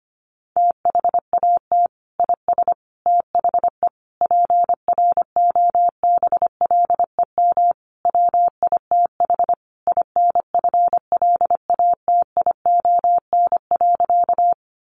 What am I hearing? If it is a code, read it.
Morse code, heard as a keyed tone, about 25 wpm, 700 Hz. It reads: THAT IS THE PROBLEM WITH INFLATION.
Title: Sentences from Top 1100 Words 25wpm